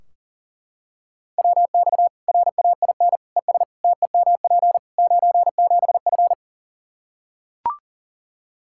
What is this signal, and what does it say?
Morse code, heard as a keyed tone, about 37 wpm, 700 Hz. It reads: WX RAIN ES TEMP 97F
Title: Practice QSO - Exchange 2 - Info - 40wpm